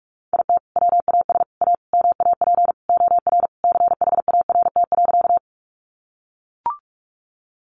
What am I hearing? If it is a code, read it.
Morse code, heard as a keyed tone, about 44 wpm, 700 Hz. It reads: IT WAS A MAP OR CHART.
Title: Sentences from Top 1300 Words 45wpm